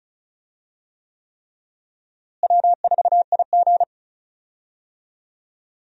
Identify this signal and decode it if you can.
Morse code, heard as a keyed tone, about 36 wpm, 700 Hz. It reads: W4IG
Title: USA Call Signs - 4 Characters - Repeated 3x - 35wpm